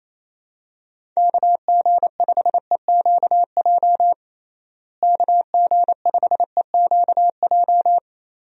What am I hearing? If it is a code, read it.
Morse code, heard as a keyed tone, about 28 wpm, 700 Hz. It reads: KG5EQJ KG5EQJ
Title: USA Call Signs - 6 Characters - Repeated 3x - 28wpm